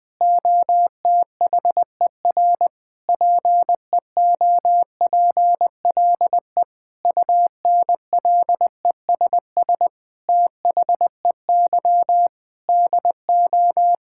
Morse code, heard as a keyed tone, about 20 wpm, 700 Hz. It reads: OTHER PEOPLE UNLESS THEY DO
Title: Sentences from Top 1000 Words - Review Part 6 - 20wpm